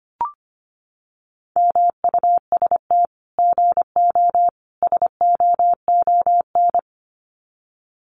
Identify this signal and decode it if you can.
Morse code, heard as a keyed tone, about 25 wpm, 700 Hz. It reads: MUST GO SOON
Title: Common QSO Phrases 25wpm